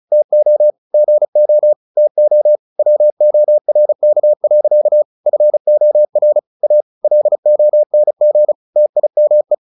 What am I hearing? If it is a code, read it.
Morse code, heard as a keyed tone, about 34 wpm, 600 Hz. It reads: TO GO TO WORK. FOR A LONG TIME
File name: the old man and the sea 2 35wpm